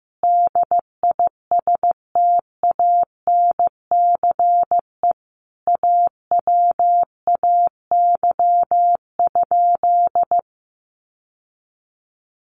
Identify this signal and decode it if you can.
Morse code, heard as a keyed tone, about 15 wpm, 700 Hz. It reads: DISTANCE AWAY?
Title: Sentences from Top 1200 Words 15wpm